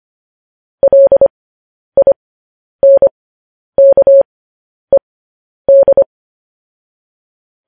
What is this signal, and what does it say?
Morse code, heard as a keyed tone, about 25 wpm, spaced as for 10 wpm, 550 Hz. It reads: LINKED
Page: news headlines in morse code at 10 wpm